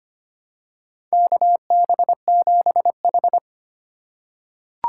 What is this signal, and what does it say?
Morse code, heard as a keyed tone, about 25 wpm, 700 Hz. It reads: KB7H
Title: K1USN SST QSO Practice - 25wpm